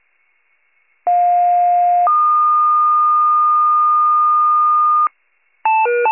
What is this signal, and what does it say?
Two-tone page, heard as a one-second tone, then a three-second tone, about 710 then 1220 Hz. It is heard from 1.1 to 5.1 s.